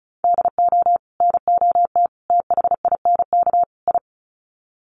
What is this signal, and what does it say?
Morse code, heard as a keyed tone, about 34 wpm, 700 Hz. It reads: DO NOT THINK I
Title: Sets of 4 Words 35wpm